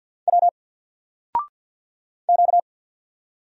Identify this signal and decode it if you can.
Morse code, heard as a keyed tone, about 48 wpm, 700 Hz. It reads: W Q
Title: Single Letter-Number 50wpm